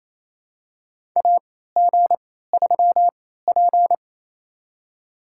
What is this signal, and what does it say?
Morse code, heard as a keyed tone, about 28 wpm, spaced as for 16 wpm, 700 Hz. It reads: AG3P
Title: USA Call Signs - 4 Characters - 3x Character Spacing - 28wpm